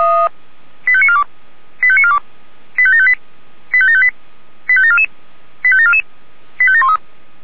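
Five-tone selective call (ZVEI), heard as quick runs of five tones, tones of about 70 ms. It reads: 85942 85942 85758 85758 85744 85744 85713
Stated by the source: ZVEI five-tone call received by SDR.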